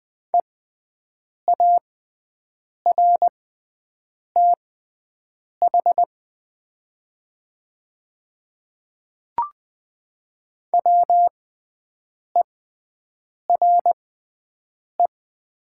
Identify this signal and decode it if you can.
Morse code, heard as a keyed tone, about 20 wpm, spaced as for 7 wpm, 690 Hz. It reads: EARTH WERE
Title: 6x Character Spacing - Top 500 Words - 20wpm